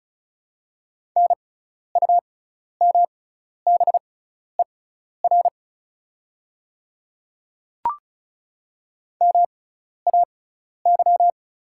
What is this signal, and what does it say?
Morse code, heard as a keyed tone, about 36 wpm, spaced as for 12 wpm, 700 Hz. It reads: NUMBER MAY
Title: 6x Character Spacing - Top 100 Words - 35wpm